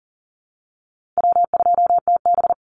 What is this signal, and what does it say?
Morse code, heard as a keyed tone, about 40 wpm, 700 Hz. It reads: W2TB
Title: Call Signs Repeated 3x - 40wpm